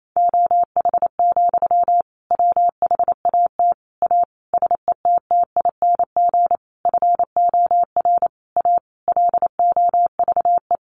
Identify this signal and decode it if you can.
Morse code, heard as a keyed tone, about 28 wpm, 700 Hz. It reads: OH, WHAT A SETTING FOR A LOVE
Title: Sentences from Top 1300 Words 28wpm